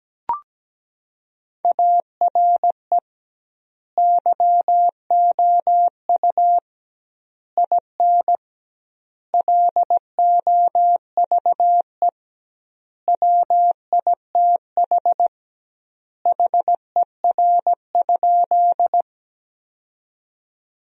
Morse code, heard as a keyed tone, about 17 wpm, 700 Hz. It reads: ARE YOU IN LOVE WITH HER?
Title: 2x Word-Spacing - Sentences from Top 400 Words - 17wpm